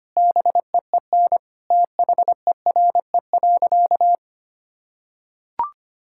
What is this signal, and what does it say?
Morse code, heard as a keyed tone, about 25 wpm, 700 Hz. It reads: BEEN THERE.
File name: practice / 25wpm / Sentences with 4-Word Phrases - 25wpm